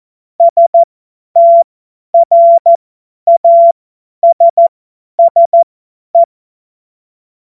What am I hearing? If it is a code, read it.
Morse code, heard as a keyed tone, about 14 wpm, spaced as for 10 wpm, 680 Hz. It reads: STRASSE
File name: Witz-23-September-2025-50-F70